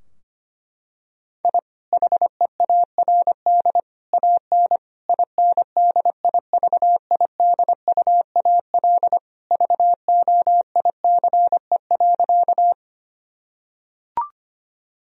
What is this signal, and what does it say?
Morse code, heard as a keyed tone, about 25 wpm, 700 Hz. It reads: I HEARD AN INDIVIDUAL VOICE.
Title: Sentences from Top 1000 Words - Review Part 4 - 25wpm